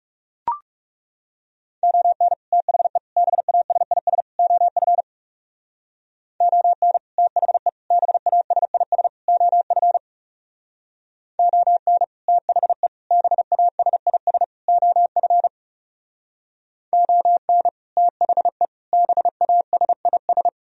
Morse code, heard as a keyed tone, about 38 wpm, 700 Hz. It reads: ON THE BASIS OF ON THE BASIS OF ON THE BASIS OF ON THE BASIS